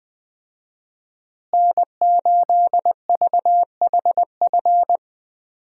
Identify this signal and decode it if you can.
Morse code, heard as a keyed tone, about 21 wpm, 700 Hz. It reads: N8VHF
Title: USA Call Signs - 5 Characters - Repeated 2x - 20wpm